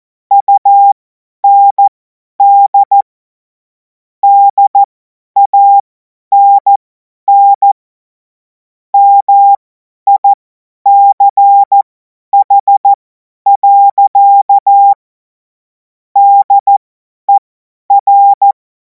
Morse code, heard as a keyed tone, about 14 wpm, spaced as for 10 wpm, 800 Hz. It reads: UND DANN MICH. DER
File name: 50 F70